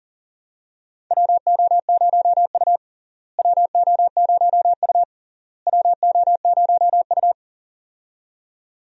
Morse code, heard as a keyed tone, about 40 wpm, 700 Hz. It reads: WO0U WO0U WO0U
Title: USA Call Signs - 4 Characters - Repeated 3x - 40wpm